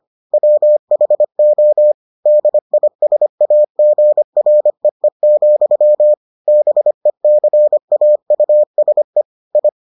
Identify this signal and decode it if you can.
Morse code, heard as a keyed tone, about 25 wpm, 600 Hz. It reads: WHO DISAGREE, BECAUSE I